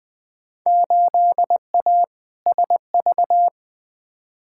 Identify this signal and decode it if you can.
Morse code, heard as a keyed tone, about 21 wpm, 700 Hz. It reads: 8A SV
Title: Field Day Exchanges Encore - 20wpm